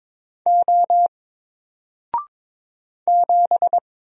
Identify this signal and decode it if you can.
Morse code, heard as a keyed tone, about 22 wpm, 700 Hz. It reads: O 7
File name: Single-Letter-Number-22wpm